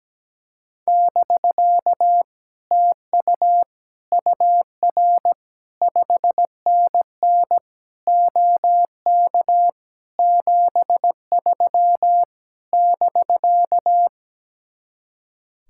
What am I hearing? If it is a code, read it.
Morse code, heard as a keyed tone, about 17 wpm, 700 Hz. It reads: <BK> TU UR 5NN OK 73 <BK>
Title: Speed-Racing Common QSO Phrases - 17wpm